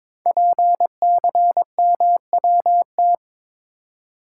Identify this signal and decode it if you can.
Morse code, heard as a keyed tone, about 22 wpm, 700 Hz. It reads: PCMWT
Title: Code Groups - 5 Letters - 22wpm